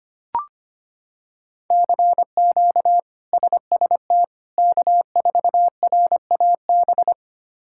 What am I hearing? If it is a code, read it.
Morse code, heard as a keyed tone, about 25 wpm, 700 Hz. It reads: CQ SST K4RAB
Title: K1USN SST QSO Practice - 25wpm